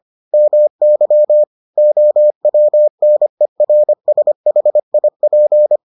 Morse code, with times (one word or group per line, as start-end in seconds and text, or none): 0.3-1.4 MY
1.8-5.8 OWNERSHIP